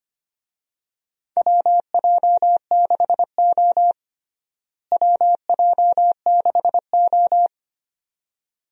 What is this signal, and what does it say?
Morse code, heard as a keyed tone, about 25 wpm, 700 Hz. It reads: WJ6O WJ6O
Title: USA Call Signs - 4 Characters - Repeated 2x - 25wpm